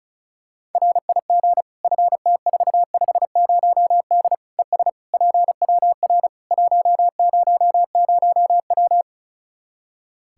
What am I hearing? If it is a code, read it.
Morse code, heard as a keyed tone, about 36 wpm, 700 Hz. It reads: RIG FT450D ES PWR 100W
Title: Practice QSO - Exchange 2 - Info - 35wpm